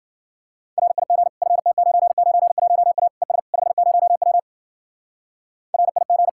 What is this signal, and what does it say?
Morse code, heard as a keyed tone, about 52 wpm, 700 Hz. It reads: RIG FT991A ES 50W RIG